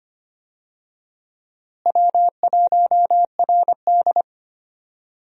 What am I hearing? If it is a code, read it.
Morse code, heard as a keyed tone, about 25 wpm, 700 Hz. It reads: W1RD
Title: USA Call Signs - 4 Characters - Repeated 2x - 25wpm